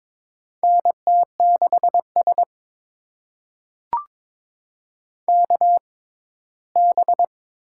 Morse code, heard as a keyed tone, about 22 wpm, spaced as for 15 wpm, 700 Hz. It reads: NT6S KB
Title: USA Call Signs - 4 Characters - 6x Character Spacing - 22wpm